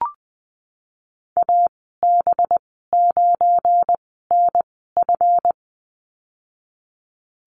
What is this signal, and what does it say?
Morse code, heard as a keyed tone, about 19 wpm, spaced as for 14 wpm, 700 Hz. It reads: AB9NF
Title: 186 Lesson 30g - Letter b - Call Signs 20wpm